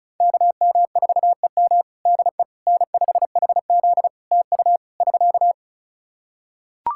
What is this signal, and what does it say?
Morse code, heard as a keyed tone, about 34 wpm, 700 Hz. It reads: KM4EM DE N5HZ TU <SK>
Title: Practice QSO - Exchange 4 - Ending - 35wpm